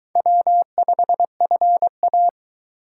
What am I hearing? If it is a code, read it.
Morse code, heard as a keyed tone, about 22 wpm, 700 Hz. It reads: W5FA